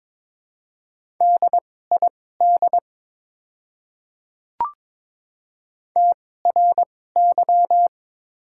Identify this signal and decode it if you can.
Morse code, heard as a keyed tone, about 22 wpm, spaced as for 16 wpm, 700 Hz. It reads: DID TRY
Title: 2x Character Spacing - Top 200 Words - 22wpm